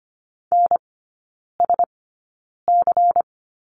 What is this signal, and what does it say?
Morse code, heard as a keyed tone, about 25 wpm, spaced as for 9 wpm, 700 Hz. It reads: NSC
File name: course / 25:10wpm / 093 Lesson 16e - Letter c - Three Characters 25wpm